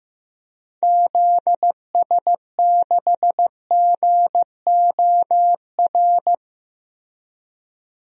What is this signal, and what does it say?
Morse code, heard as a keyed tone, about 15 wpm, 700 Hz. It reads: ZS6GOR
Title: DXCC-100 Call Signs - 6 Characters - Repeated 3x - 15wpm